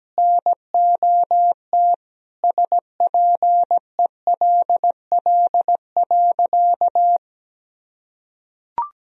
Morse code, heard as a keyed tone, about 17 wpm, 700 Hz. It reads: NOT SPELL.
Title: Sentences from Top 1000 Words - Review Part 1 - 17wpm